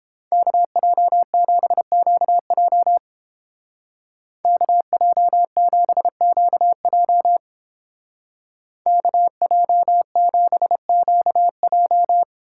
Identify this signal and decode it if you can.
Morse code, heard as a keyed tone, about 31 wpm, 700 Hz. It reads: KJ7QJ KJ7QJ KJ7QJ